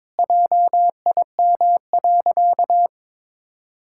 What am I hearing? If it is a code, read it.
Morse code, heard as a keyed tone, about 21 wpm, 700 Hz. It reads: JIM.
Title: Pangram Challenge v2 - 22wpm